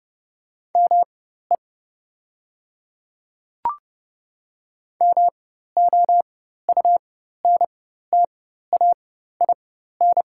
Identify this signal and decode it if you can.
Morse code, heard as a keyed tone, about 30 wpm, spaced as for 14 wpm, 700 Hz. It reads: ME MOUNTAIN
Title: 4x Character Spacing - Top 500 Words - 30wpm